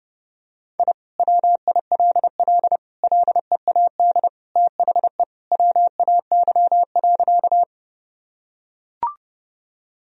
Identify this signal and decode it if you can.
Morse code, heard as a keyed tone, about 32 wpm, 700 Hz. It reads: I WILL LEAD THE WAY.
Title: Sentences from Top 1000 Words - Review Part 2 - 30wpm